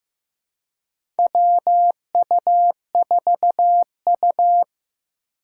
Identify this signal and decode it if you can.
Morse code, heard as a keyed tone, about 15 wpm, 700 Hz. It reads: WU4U
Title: USA Call Signs - 4 Characters - 3x Character Spacing - 15wpm